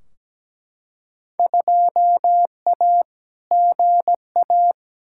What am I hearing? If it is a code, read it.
Morse code, heard as a keyed tone, about 17 wpm, 700 Hz. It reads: 2A GA
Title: Field Day Exchanges - 17wpm